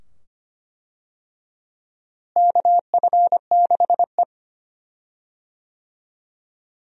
Morse code, heard as a keyed tone, about 25 wpm, 700 Hz. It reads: KF6E
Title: USA Call Signs - 4 Characters - Repeated 2x - 25wpm